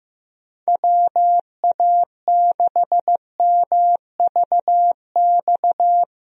Morse code, heard as a keyed tone, about 15 wpm, 700 Hz. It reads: WA6MVX
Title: USA Call Signs - 6 Characters - Repeated 2x - 15wpm